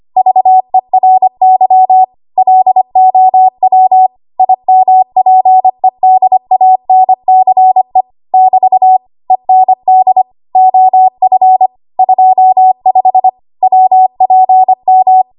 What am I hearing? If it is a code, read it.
Morse code, heard as a keyed tone, about 25 wpm, 750 Hz. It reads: VERY LOW IMPEDANCE <BT> END OF 25 WPM
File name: Archive / 25 WPM / 25WPM